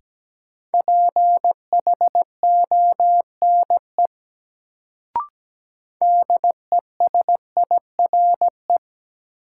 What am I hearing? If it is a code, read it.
Morse code, heard as a keyed tone, about 17 wpm, 700 Hz. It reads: PHONE DESIRE
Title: Sentences from Top 1100 Words - New Word Drill Alt - 17wpm